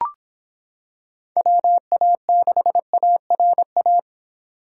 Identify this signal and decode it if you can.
Morse code, heard as a keyed tone, about 25 wpm, 700 Hz. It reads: WA6ARA